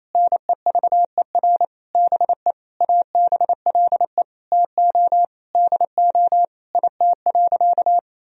Morse code, heard as a keyed tone, about 28 wpm, 700 Hz. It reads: NEVER BE ABLE TO DO IT.
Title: Sentences from Top 1000 Words - Review Part 2 - 28wpm